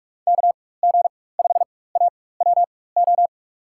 Morse code, heard as a keyed tone, about 42 wpm, spaced as for 20 wpm, 700 Hz. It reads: KG5AWO
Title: USA Call Signs - 6 Characters - 4x Character Spacing - 45wpm